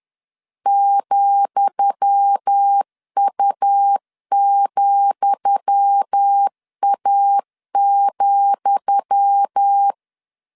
Morse code, heard as a keyed tone, about 10 wpm, 800 Hz. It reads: ,U,A,
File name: LEZIONI / 10 SIGNS / 10D COMA 10wpm 3013